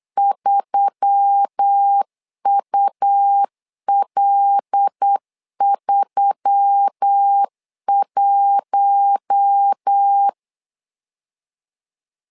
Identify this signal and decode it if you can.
Morse code, heard as a keyed tone, about 8 wpm, 800 Hz. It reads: 3UL31
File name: LEZIONI / 5 LBVC / 5 LB 8wpm 2816